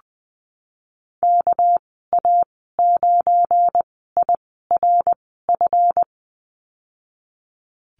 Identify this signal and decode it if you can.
Morse code, heard as a keyed tone, about 20 wpm, 700 Hz. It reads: K A 9 I R F
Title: 238 Lesson 38g - Letter q - Call Signs 20wpm